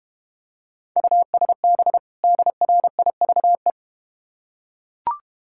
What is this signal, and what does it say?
Morse code, heard as a keyed tone, about 31 wpm, 700 Hz. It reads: USB DRIVE
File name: Easy Words - Speed-Racing with Hints - 25wpm